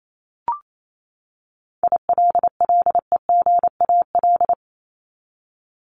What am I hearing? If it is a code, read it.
Morse code, heard as a keyed tone, about 28 wpm, 700 Hz. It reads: ILLEGAL
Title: Top 1751-2000 Words 28wpm